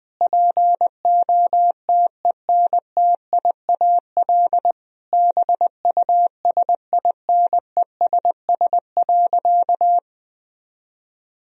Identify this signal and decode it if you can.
Morse code, heard as a keyed tone, about 21 wpm, 700 Hz. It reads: POTENTIAL BUSINESS.
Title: Sentences from Top 1000 Words - Review Part 4 - 20wpm